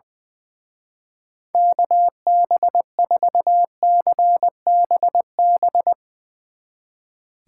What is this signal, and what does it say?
Morse code, heard as a keyed tone, about 20 wpm, 700 Hz. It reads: KB4CBB
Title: Speed Racing - USA 2x3 Call Signs 20wpm